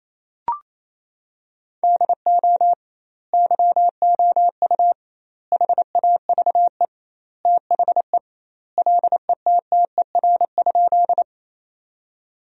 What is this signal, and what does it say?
Morse code, heard as a keyed tone, about 28 wpm, 700 Hz. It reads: DO YOU HAVE THE LETTER?
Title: 2x Word-Spacing - Sentences from Top 300 Words - 28wpm